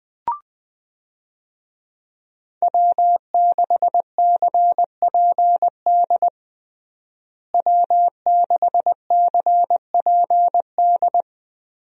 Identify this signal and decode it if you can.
Morse code, heard as a keyed tone, about 21 wpm, 700 Hz. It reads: W6CPD W6CPD
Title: USA Call Signs - 5 Characters - Repeated 3x - 20wpm